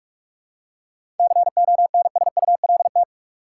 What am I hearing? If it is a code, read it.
Morse code, heard as a keyed tone, about 46 wpm, 700 Hz. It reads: CONSULT
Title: Seven Letter Words - 45wpm